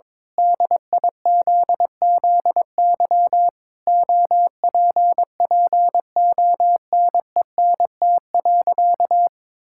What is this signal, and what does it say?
Morse code, heard as a keyed tone, about 22 wpm, 700 Hz. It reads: DIZZY OPPONENT.
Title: Pangram Challenge v2 - 22wpm